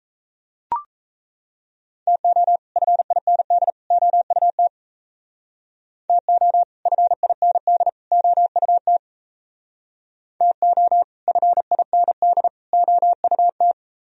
Morse code, heard as a keyed tone, about 36 wpm, 700 Hz. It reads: TO FIND OUT TO FIND OUT TO FIND OUT